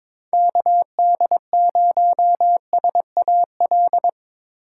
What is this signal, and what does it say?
Morse code, heard as a keyed tone, about 22 wpm, 700 Hz. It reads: KD0SAL
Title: USA Call Signs - 6 Characters - 3x Character Spacing - 22wpm